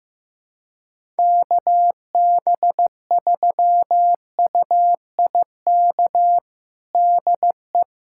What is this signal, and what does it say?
Morse code, heard as a keyed tone, about 15 wpm, 700 Hz. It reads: KB3UIK DE
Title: USA Callsign DE Callsign - 6 Characters - 15wpm